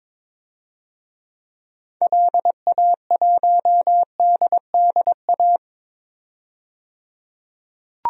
Morse code, heard as a keyed tone, about 22 wpm, 700 Hz. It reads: LA1DDA
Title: DXCC-100 Call Signs - 6 Characters - 22wpm